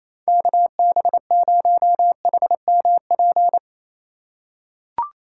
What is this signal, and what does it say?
Morse code, heard as a keyed tone, about 27 wpm, 700 Hz. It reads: KB0HMP
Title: USA Call Signs - 3x Character Spacing - 28wpm